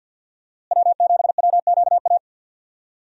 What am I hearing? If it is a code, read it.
Morse code, heard as a keyed tone, about 48 wpm, 700 Hz. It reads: W7WQA
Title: USA Call Signs - 5 Characters - Repeated 3x - 50wpm